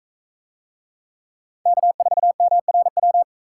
Morse code, heard as a keyed tone, about 38 wpm, 700 Hz. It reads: K4MRW